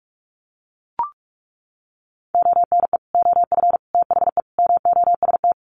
Morse code, heard as a keyed tone, about 44 wpm, 700 Hz. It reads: ONE OF THE MOST
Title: Sets of 4 Words 45wpm